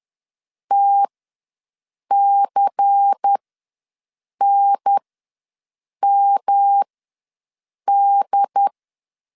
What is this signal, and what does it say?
Morse code, heard as a keyed tone, about 10 wpm, spaced as for 6 wpm, 800 Hz. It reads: TCNMD
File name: LEZIONI / 5 LBVC / 5 LBVC ws 10wpm 3733